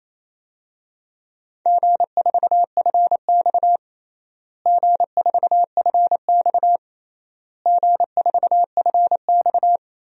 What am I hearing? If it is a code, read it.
Morse code, heard as a keyed tone, about 28 wpm, 700 Hz. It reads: G4FX G4FX G4FX
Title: DXCC-100 Call Signs - 4 Characters - Repeated 3x - 28wpm